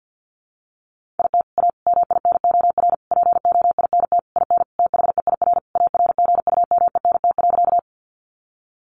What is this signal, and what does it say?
Morse code, heard as a keyed tone, about 48 wpm, 700 Hz. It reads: IT A MINOR POINT IN THEIR ARGUMENT.
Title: Sentences from Top 1500 Words 50wpm